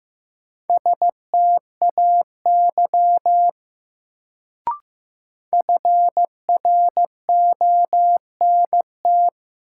Morse code, heard as a keyed tone, about 15 wpm, 700 Hz. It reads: STAY FRONT
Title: Sentences from Top 500 Words - New Word Drill - 15wpm